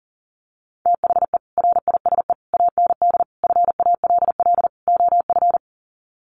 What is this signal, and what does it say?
Morse code, heard as a keyed tone, about 40 wpm, 700 Hz. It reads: THE RISE AND FALL OF
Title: Sets of 5 Words Encore2 40wpm